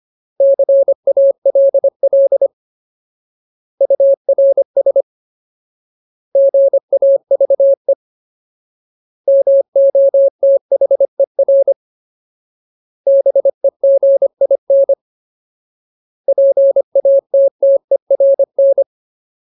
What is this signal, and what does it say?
Morse code, heard as a keyed tone, about 25 wpm, 550 Hz. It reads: CALL URS GAVE MOTHER BEGIN PATTERN
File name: Ultimate Word List / 25WPM Ultimate